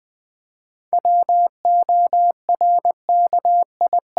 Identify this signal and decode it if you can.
Morse code, heard as a keyed tone, about 20 wpm, 700 Hz. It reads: WORKI
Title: 7 Letter Words - First 3 Letters Given - 20wpm